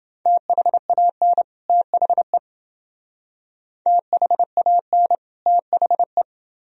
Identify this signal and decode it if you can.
Morse code, heard as a keyed tone, about 29 wpm, 700 Hz. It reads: THAN THE THAN THE